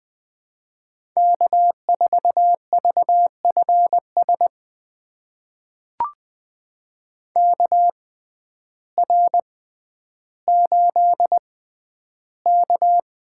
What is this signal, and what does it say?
Morse code, heard as a keyed tone, about 20 wpm, spaced as for 11 wpm, 700 Hz. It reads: K4VFS KR8K
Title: USA Call Signs - 6x Character Spacing - 20wpm